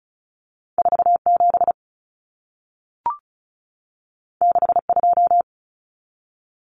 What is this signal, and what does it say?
Morse code, heard as a keyed tone, about 36 wpm, 700 Hz. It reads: 47 62
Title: Numbers - 2-Digit 35wpm